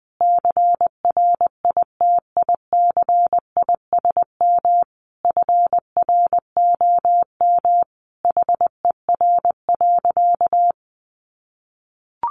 Morse code, heard as a keyed tone, about 20 wpm, 700 Hz. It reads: CRITICISM FROM HER.
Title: Sentences from Top 1300 Words 20wpm